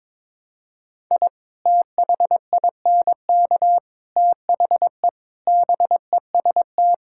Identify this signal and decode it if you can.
Morse code, heard as a keyed tone, about 22 wpm, 700 Hz. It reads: I THINK THE BEST
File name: practice / 22wpm / Sets of 8 Words - 22wpm